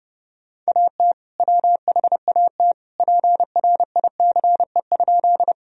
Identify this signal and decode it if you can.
Morse code, heard as a keyed tone, about 32 wpm, 700 Hz. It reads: AT WHAT PRICE?
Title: Sentences from Top 1000 Words - Review Part 4 - 30wpm